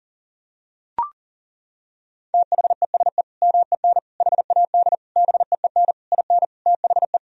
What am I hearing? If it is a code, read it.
Morse code, heard as a keyed tone, about 40 wpm, 700 Hz. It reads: THESE MEN HAD BEEN IN THE
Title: Sentences from Top 1000 Words - Review Part 1 - 40wpm